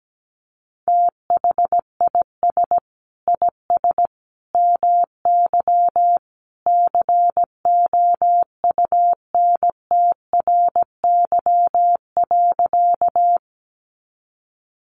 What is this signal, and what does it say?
Morse code, heard as a keyed tone, about 17 wpm, 700 Hz. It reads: THIS IS MY COUNTRY.
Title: Sentences from Top 300 Words 17wpm